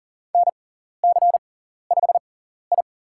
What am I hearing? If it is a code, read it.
Morse code, heard as a keyed tone, about 40 wpm, spaced as for 14 wpm, 700 Hz. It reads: NC5I
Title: USA Call Signs - 4 Characters - 6x Character Spacing - 40wpm